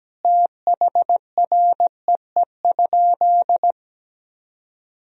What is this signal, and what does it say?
Morse code, heard as a keyed tone, about 17 wpm, 700 Hz. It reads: THREE?
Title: Sentences from Top 1000 Words - Review Part 2 - 17wpm